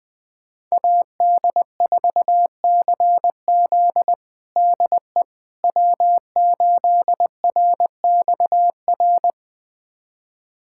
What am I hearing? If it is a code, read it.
Morse code, heard as a keyed tone, about 20 wpm, 700 Hz. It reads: AD4CZ DE W8RXR
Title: USA Callsign DE Callsign - 5 Characters - 20wpm